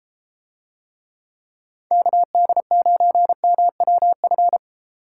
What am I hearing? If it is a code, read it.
Morse code, heard as a keyed tone, about 32 wpm, 700 Hz. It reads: KD9MWF